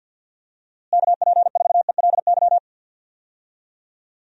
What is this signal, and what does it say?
Morse code, heard as a keyed tone, about 51 wpm, 700 Hz. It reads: KP4ELY
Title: USA Call Signs - 6 Characters - Repeated 3x - 50wpm